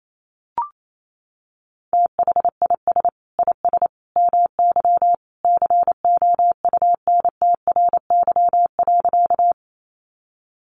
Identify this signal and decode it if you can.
Morse code, heard as a keyed tone, about 28 wpm, 700 Hz. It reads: THIS IS MY COUNTRY.
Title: Sentences from Top 300 Words 28wpm